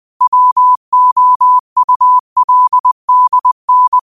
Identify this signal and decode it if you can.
Morse code, heard as a keyed tone, about 20 wpm, 1000 Hz. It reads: WOULDN
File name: qotdfiles / 20wpm